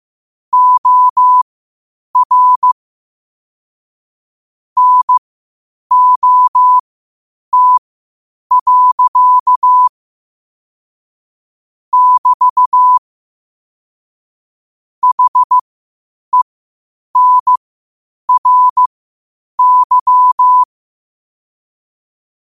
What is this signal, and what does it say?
Morse code, heard as a keyed tone, about 15 wpm, spaced as for 8 wpm, 1000 Hz. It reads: OR NOT. <BT> HENRY
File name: qotdfiles / 8wpm